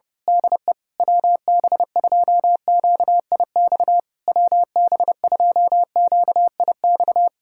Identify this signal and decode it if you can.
Morse code, heard as a keyed tone, about 29 wpm, 700 Hz. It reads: DE WB2QIX WB2QIX
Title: Speed-Racing Common QSO Phrases - 25wpm